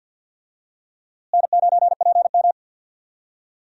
Morse code, heard as a keyed tone, about 47 wpm, 700 Hz. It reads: N9PM
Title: USA Call Signs - 4 Characters - Repeated 2x - 50wpm